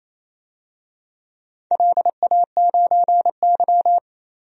Morse code, heard as a keyed tone, about 28 wpm, 700 Hz. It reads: LA9Y
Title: DXCC-100 Call Signs - 4 Characters - 28wpm